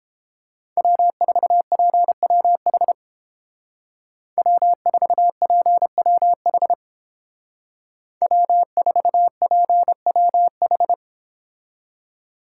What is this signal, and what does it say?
Morse code, heard as a keyed tone, about 31 wpm, 700 Hz. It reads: W4PWH W4PWH W4PWH